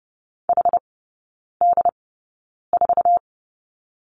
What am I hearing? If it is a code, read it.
Morse code, heard as a keyed tone, about 30 wpm, spaced as for 9 wpm, 700 Hz. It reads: HD4
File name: Lesson 17e - Letter u - Three Characters 30wpm